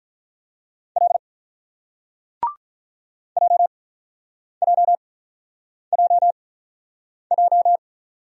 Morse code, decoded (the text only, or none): L J J J J